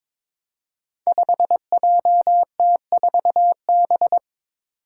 Morse code, heard as a keyed tone, about 22 wpm, 700 Hz. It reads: 5JT4B
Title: Five Challenging Letter-Number Combinations 22wpm